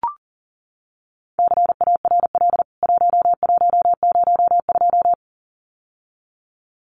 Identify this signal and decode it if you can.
Morse code, heard as a keyed tone, about 40 wpm, 700 Hz. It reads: CARL 1102
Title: CWT Exchange 40wpm